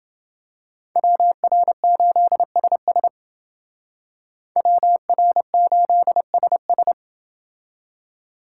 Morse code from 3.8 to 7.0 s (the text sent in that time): WR8SS